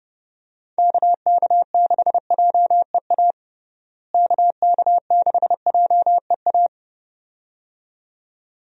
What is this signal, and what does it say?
Morse code, heard as a keyed tone, about 30 wpm, 700 Hz. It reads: KK6JEA KK6JEA
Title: USA Call Signs - 6 Characters - Repeated 2x - 30wpm